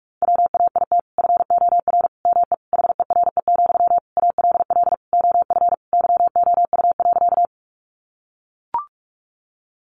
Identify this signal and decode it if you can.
Morse code, heard as a keyed tone, about 44 wpm, 700 Hz. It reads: WAIT FOR ME HERE, ALL OF YOU.
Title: Sentences from Top 500 Words 45wpm